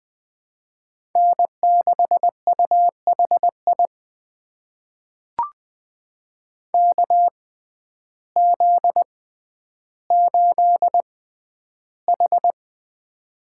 Morse code, heard as a keyed tone, about 20 wpm, spaced as for 11 wpm, 700 Hz. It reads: N6UHI KZ8H
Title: USA Call Signs - 6x Character Spacing - 20wpm